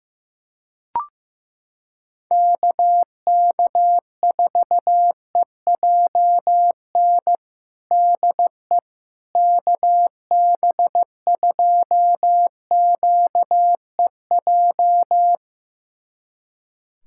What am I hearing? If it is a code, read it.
Morse code, heard as a keyed tone, about 16 wpm, 700 Hz. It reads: KK4EJN DE KB2QEJ
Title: USA Callsign DE Callsign 15wpm